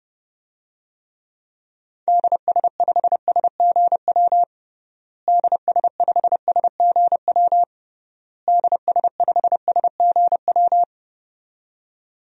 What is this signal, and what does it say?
Morse code, heard as a keyed tone, about 31 wpm, 700 Hz. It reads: DS5SGW DS5SGW DS5SGW
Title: DXCC-100 Call Signs - 6 Characters - Repeated 3x - 30wpm